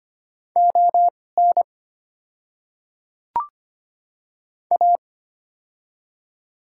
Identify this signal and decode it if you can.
Morse code, heard as a keyed tone, about 25 wpm, spaced as for 18 wpm, 700 Hz. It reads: ON A
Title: 2x Character Spacing - Top 100 Words - 25wpm